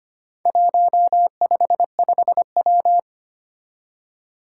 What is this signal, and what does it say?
Morse code, heard as a keyed tone, about 25 wpm, 700 Hz. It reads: 155W
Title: Four Challenging Letter-Number Combinations 25wpm